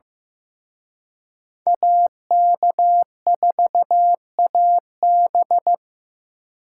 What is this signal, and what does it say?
Morse code, heard as a keyed tone, about 18 wpm, 700 Hz. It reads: AK4AB
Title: Speed Racing - USA Call Signs 15wpm